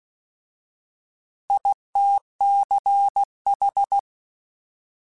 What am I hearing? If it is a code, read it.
Morse code, heard as a keyed tone, about 16 wpm, 800 Hz. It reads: ITCH